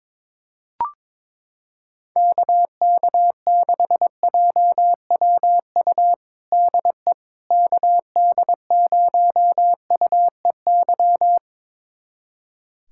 Morse code, heard as a keyed tone, about 22 wpm, 700 Hz. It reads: KK6JWU DE KD0UEY
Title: USA Callsign DE Callsign 22wpm